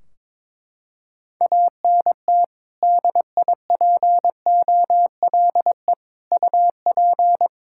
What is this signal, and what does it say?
Morse code, heard as a keyed tone, about 22 wpm, 700 Hz. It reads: ANT DIPOLE UP
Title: Practice QSO - Exchange 2 - Info - 22wpm